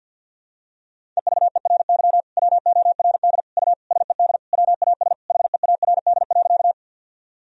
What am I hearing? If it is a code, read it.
Morse code, heard as a keyed tone, about 48 wpm, 700 Hz. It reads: EVERY WORD U SED WAS HEARD.
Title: Sentences from Top 400 Words and Abbreviations 50wpm